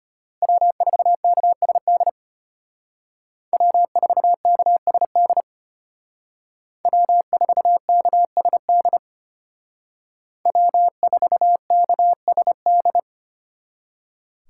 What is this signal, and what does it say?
Morse code, heard as a keyed tone, about 32 wpm, 700 Hz. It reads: W4KSD W4KSD W4KSD W4KSD